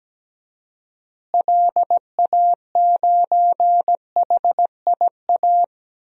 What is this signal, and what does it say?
Morse code, heard as a keyed tone, about 17 wpm, 700 Hz. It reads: LA9HIA
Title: DXCC-100 Call Signs - 6 Characters - Repeated 3x - 17wpm